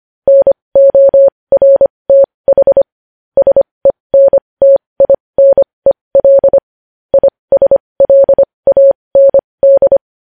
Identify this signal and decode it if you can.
Morse code, heard as a keyed tone, about 25 wpm, spaced as for 20 wpm, 550 Hz. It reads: NORTH SENTINEL ISLAND
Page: News Headlines in Morse Code at 20 WPM